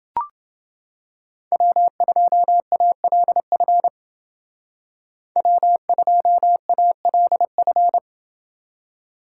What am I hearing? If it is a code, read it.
Morse code, heard as a keyed tone, about 30 wpm, 700 Hz. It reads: W2ALF W2ALF